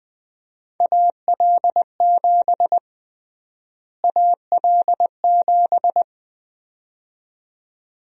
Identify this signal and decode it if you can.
Morse code, heard as a keyed tone, about 20 wpm, 700 Hz. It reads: AL7 AL7
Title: USA Call Sign Prefixes - Repeated 2x - 20wpm